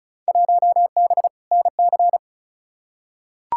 Morse code, heard as a keyed tone, about 36 wpm, 700 Hz. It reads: 1B NC
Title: Field Day Exchanges - 35wpm